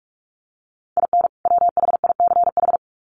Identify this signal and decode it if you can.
Morse code, heard as a keyed tone, about 44 wpm, 700 Hz. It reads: IN WHICH
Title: Sets of 2 Words 45wpm